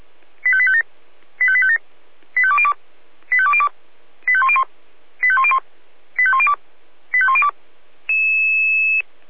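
Five-tone selective call (ZVEI), heard as quick runs of five tones, tones of about 70 ms. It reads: 85857 85857 85202 85202 85101 85101 85102 85102
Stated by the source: ZVEI five-tone selective call received on SDR, from the Feuerwehr Orlamünde audio server.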